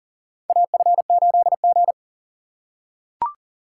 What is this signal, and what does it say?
Morse code, heard as a keyed tone, about 40 wpm, 700 Hz. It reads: AF8G
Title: USA Call Signs - 4 Characters - 3x Character Spacing - 40wpm